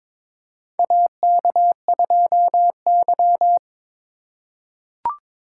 Morse code, heard as a keyed tone, about 22 wpm, 700 Hz. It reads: AK2Y
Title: USA Call Signs - 4 Characters - 3x Character Spacing - 22wpm